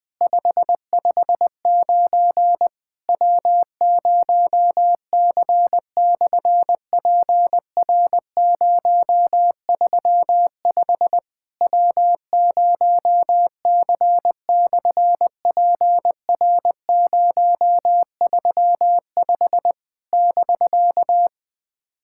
Morse code, heard as a keyed tone, about 20 wpm, 700 Hz. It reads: 559 W0C/PR035 W0C/PR035 <BK>
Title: SOTA QSO Practice 20wpm